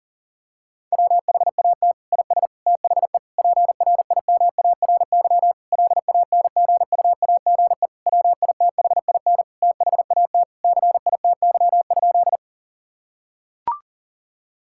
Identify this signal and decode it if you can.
Morse code, heard as a keyed tone, about 40 wpm, 700 Hz. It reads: WHAT IS THE PRIMARY LANGUAGE WITHIN THAT CITY?
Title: Sentences from Top 1000 Words - Review Part 5 - 40wpm